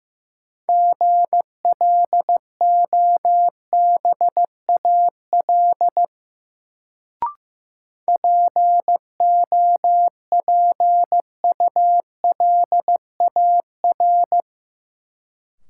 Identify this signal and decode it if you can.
Morse code, heard as a keyed tone, about 15 wpm, 700 Hz. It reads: GLOBAL POPULAR
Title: Sentences from Top 1200 Words - New Word Drill Alt - 15wpm